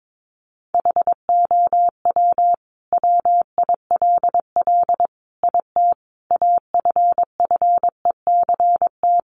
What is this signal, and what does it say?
Morse code, heard as a keyed tone, about 22 wpm, 700 Hz. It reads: HOW WILL IT AFFECT
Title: Sentences from Top 900 Words 22wpm